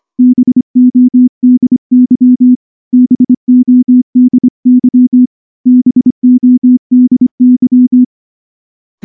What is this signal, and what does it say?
Morse code, heard as a keyed tone, about 23 wpm, 250 Hz. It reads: BODY BODY BODY
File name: morse / morse-training-25WPM